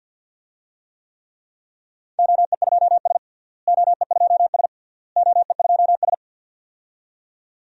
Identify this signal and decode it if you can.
Morse code, heard as a keyed tone, about 47 wpm, 700 Hz. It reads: OE2S OE2S OE2S
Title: DXCC-100 Call Signs - 4 Characters - Repeated 3x - 50wpm